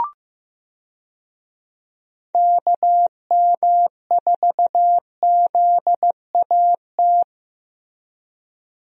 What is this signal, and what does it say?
Morse code, heard as a keyed tone, about 16 wpm, 700 Hz. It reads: KM4ZAT
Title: USA Call Signs - 6 Characters - Repeated 3x - 15wpm